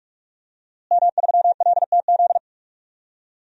Morse code, heard as a keyed tone, about 41 wpm, 700 Hz. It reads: M3LTZ
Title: DXCC-100 Call Signs - 5 Characters - Repeated 3x - 45wpm